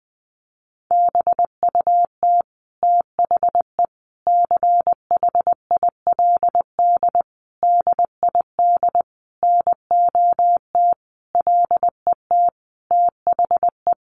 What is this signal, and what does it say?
Morse code, heard as a keyed tone, about 20 wpm, 700 Hz. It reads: BUT THE CHILD DID NOT LET THE
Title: Sentences from Top 1400 Words 20wpm